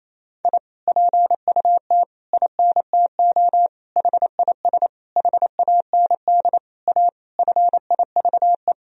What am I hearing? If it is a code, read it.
Morse code, heard as a keyed tone, about 28 wpm, 700 Hz. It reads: I PUT INTO HIS HAND A FIVE
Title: Sentences from Top 1000 Words - Review Part 2 - 28wpm